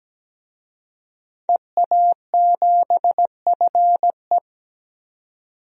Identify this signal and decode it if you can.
Morse code, heard as a keyed tone, about 17 wpm, 700 Hz. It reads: EA7FE
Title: DXCC-100 Call Signs - 5 Characters - 17wpm